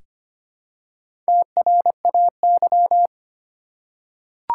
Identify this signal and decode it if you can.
Morse code, heard as a keyed tone, about 25 wpm, 700 Hz. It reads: TRAY
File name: practice / 25wpm / Easy Words with Hints - 25wpm